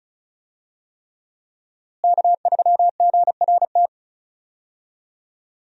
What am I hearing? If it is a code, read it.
Morse code, heard as a keyed tone, about 34 wpm, 700 Hz. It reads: K3GRT
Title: USA Call Signs - 5 Characters - Repeated 3x - 35wpm